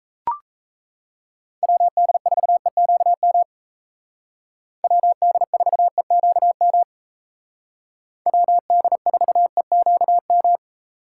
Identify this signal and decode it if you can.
Morse code, heard as a keyed tone, about 33 wpm, 700 Hz. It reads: WD4EQM WD4EQM WD4EQM
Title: Speed Racing - USA 2x3 Call Signs 28wpm